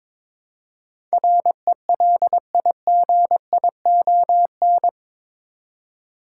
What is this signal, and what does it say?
Morse code, heard as a keyed tone, about 22 wpm, 700 Hz. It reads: RELIGION
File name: practice / 22wpm / Eight Letter Words - 22wpm